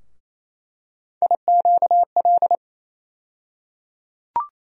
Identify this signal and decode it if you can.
Morse code, heard as a keyed tone, about 28 wpm, 700 Hz. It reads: IQL
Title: USA Call Sign Postfixes - 3 Letters 28wpm